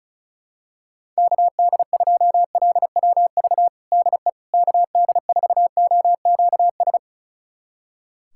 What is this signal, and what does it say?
Morse code, heard as a keyed tone, about 36 wpm, 700 Hz. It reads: KD2LWV DE KD4OQS
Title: USA Callsign DE Callsign - 6 Characters - 35wpm